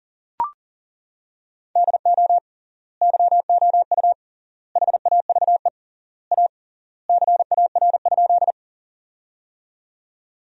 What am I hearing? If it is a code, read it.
Morse code, heard as a keyed tone, about 40 wpm, 700 Hz. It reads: DO YOU HAVE A CAR?
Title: 3x Word-Spacing - Sentences from Top 300 Words - 40wpm